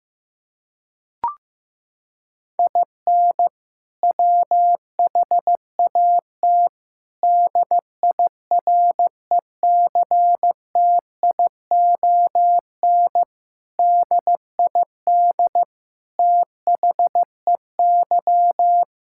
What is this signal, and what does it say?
Morse code, heard as a keyed tone, about 16 wpm, 700 Hz. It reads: IN WHAT DIRECTION DID THEY
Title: Sentences from Top 1000 Words - Review Part 5 - 15wpm